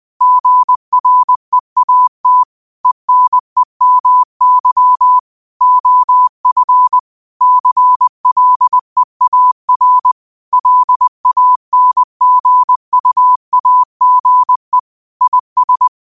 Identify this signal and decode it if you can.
Morse code, heard as a keyed tone, about 20 wpm, 1000 Hz. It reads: GREAT ENEMY OF CLEAR LANGUAGE IS
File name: qotdfiles / 20wpm